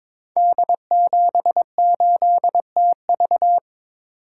Morse code, heard as a keyed tone, about 22 wpm, 690 Hz. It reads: D78TV
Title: Code Groups - 5 Letter:Numbers - 22wpm